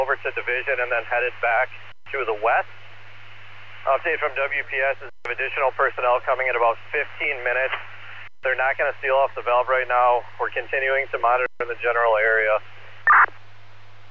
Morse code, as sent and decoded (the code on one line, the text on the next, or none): none